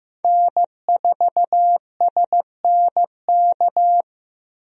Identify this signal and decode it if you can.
Morse code, heard as a keyed tone, about 15 wpm, 700 Hz. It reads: N4SNK
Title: USA Call Signs - 5 Characters - 15wpm